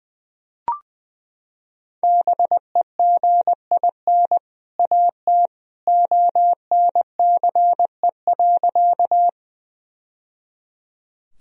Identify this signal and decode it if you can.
Morse code, heard as a keyed tone, about 20 wpm, 700 Hz. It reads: BEGIN AT ONCE.
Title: Sentences from Top 1000 Words - Review Part 1 - 20wpm